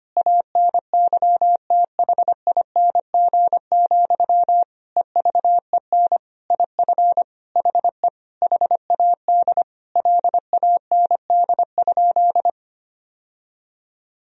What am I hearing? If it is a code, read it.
Morse code, heard as a keyed tone, about 25 wpm, 700 Hz. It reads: ANYTHING, EVEN IF HE HAD LAND?
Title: Sentences from Top 1000 Words - Review Part 1 - 25wpm